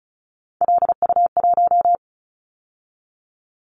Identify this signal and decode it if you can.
Morse code, heard as a keyed tone, about 36 wpm, 690 Hz. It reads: LU1
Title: Callsign Prefixes for Top 50 DXCC - 35wpm